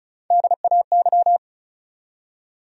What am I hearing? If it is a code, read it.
Morse code, heard as a keyed tone, about 48 wpm, spaced as for 42 wpm, 700 Hz. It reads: DAY
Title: Speed-Racing Top 100 Words 35wpm